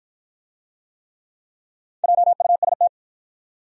Morse code, heard as a keyed tone, about 53 wpm, 700 Hz. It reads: JUST